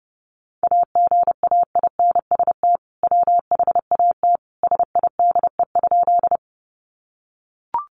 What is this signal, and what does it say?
Morse code, heard as a keyed tone, about 30 wpm, 700 Hz. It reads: AGAINST WHAT SIDE?
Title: Sentences from Top 400 Words 30wpm